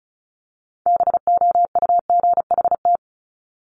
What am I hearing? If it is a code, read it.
Morse code, heard as a keyed tone, about 36 wpm, 700 Hz. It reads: BOUGHT
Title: Multi-letter Phonemes Repeated 3x 35wpm